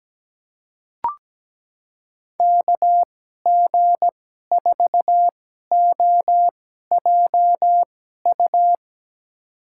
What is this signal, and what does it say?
Morse code, heard as a keyed tone, about 17 wpm, spaced as for 12 wpm, 700 Hz. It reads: KG4OJU
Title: USA Call Signs - 2x Character Spacing - 17wpm